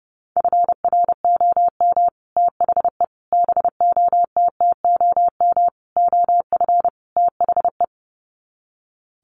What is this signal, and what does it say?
Morse code, heard as a keyed tone, about 30 wpm, 700 Hz. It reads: FROM THE BOTTOM OF THE
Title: Sets of 5 Words Encore2 30wpm